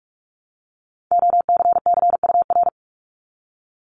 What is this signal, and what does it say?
Morse code, heard as a keyed tone, about 46 wpm, 700 Hz. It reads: OCCUR
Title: Multi-letter Phonemes 45wpm